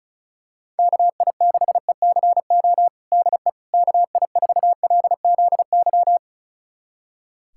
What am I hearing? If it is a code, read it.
Morse code, heard as a keyed tone, about 36 wpm, 700 Hz. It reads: KI6ECO DE KI4LZY
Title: USA Callsign DE Callsign - 6 Characters - 35wpm